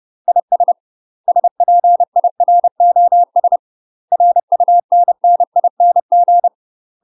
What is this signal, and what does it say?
Morse code, heard as a keyed tone, about 30 wpm, 700 Hz. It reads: IS SPIROS RUNNING
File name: UK MASTER QSOs TRAINING/30WPM